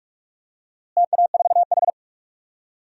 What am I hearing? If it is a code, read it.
Morse code, heard as a keyed tone, about 42 wpm, 700 Hz. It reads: TA4H
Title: DXCC-100 Call Signs - 4 Characters - Repeated 3x - 45wpm